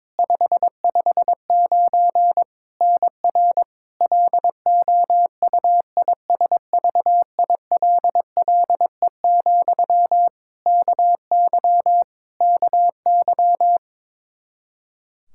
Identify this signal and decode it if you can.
Morse code, heard as a keyed tone, about 22 wpm, 700 Hz. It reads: 559 NR LOUISVILLE, KY KY
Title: Practice QSO - Exchange 1 - Beginning - 22wpm